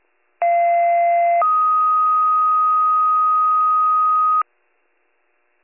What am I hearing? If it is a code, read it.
Two-tone page, heard as a one-second tone, then a three-second tone, about 690 then 1250 Hz, with noise. It is heard from 0.4 to 4.4 s.